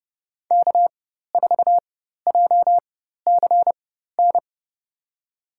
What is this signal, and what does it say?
Morse code, heard as a keyed tone, about 29 wpm, spaced as for 14 wpm, 700 Hz. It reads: K4JCN
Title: USA Call Signs - 5 Characters - 4x Character Spacing - 30wpm